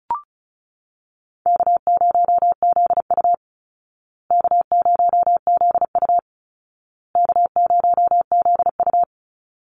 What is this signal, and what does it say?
Morse code, heard as a keyed tone, about 34 wpm, 700 Hz. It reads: K0ZU K0ZU K0ZU
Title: Easy Call Signs Repeated 3x - 35wpm